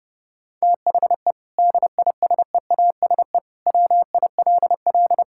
Morse code, heard as a keyed tone, about 30 wpm, 700 Hz. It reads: THE DISEASE WILL
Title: Sentences from Top 1000 Words - Review Part 6 - 30wpm